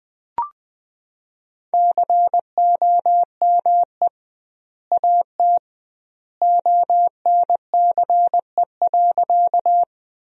Morse code, heard as a keyed tone, about 20 wpm, 700 Hz. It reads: COME AT ONCE.
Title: 2x Word-Spacing - Sentences from Top 300 Words - 20wpm